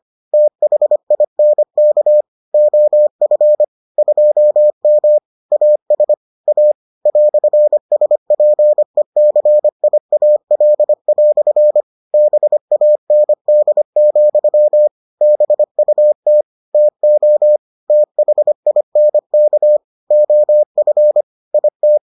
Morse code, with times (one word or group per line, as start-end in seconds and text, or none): 0.3-2.2 THINK
2.5-3.6 OF
4.0-5.2 2M
5.5-6.1 AS
6.5-6.7 A
7.1-11.8 "SPECIAL"
12.1-14.9 BAND,
15.2-16.4 BUT
16.7-17.6 TO
17.9-19.8 THINK
20.1-21.2 OF
21.5-22.0 IT